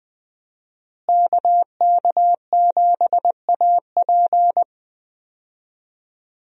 Morse code, heard as a keyed tone, about 20 wpm, 700 Hz. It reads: KK7AP
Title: USA Call Signs - 5 Characters - Repeated 3x - 20wpm